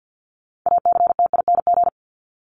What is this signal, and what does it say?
Morse code, heard as a keyed tone, about 50 wpm, 700 Hz. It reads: ACTING
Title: ING Suffix - Top 100 Words with Base Words 50wpm